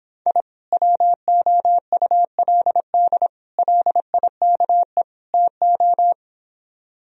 Morse code, heard as a keyed tone, about 26 wpm, 700 Hz. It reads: I WOULD LIKE TO